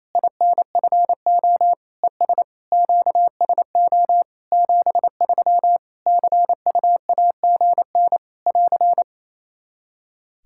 Morse code, heard as a keyed tone, about 28 wpm, 700 Hz. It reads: INFO ES QSO 73 CUAGN <AR>
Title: Practice QSO - Exchange 4 - Ending - 28wpm